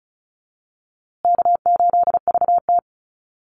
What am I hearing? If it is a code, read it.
Morse code, heard as a keyed tone, about 34 wpm, 700 Hz. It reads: K8VT
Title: USA Call Signs Repeated 2x 35wpm